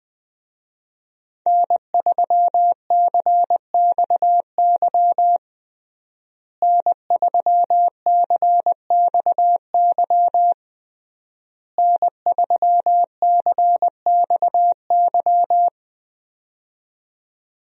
Morse code, heard as a keyed tone, about 20 wpm, 700 Hz. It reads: N3CXY N3CXY N3CXY
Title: USA Call Signs - 5 Characters - Repeated 3x - 20wpm